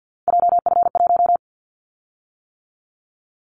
Morse code, heard as a keyed tone, about 50 wpm, 700 Hz. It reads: JF1
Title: Callsign Prefixes for Top 25 DXCC - 50wpm